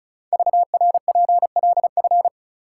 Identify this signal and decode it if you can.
Morse code, heard as a keyed tone, about 34 wpm, 700 Hz. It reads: VRPLF
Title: Code Groups - 5 Letters - 35wpm